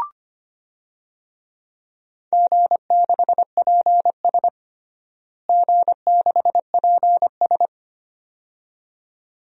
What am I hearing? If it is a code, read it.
Morse code, heard as a keyed tone, about 25 wpm, 700 Hz. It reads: G6PS G6PS
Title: DXCC-100 Call Signs - 4 Characters - Repeated 2x - 25wpm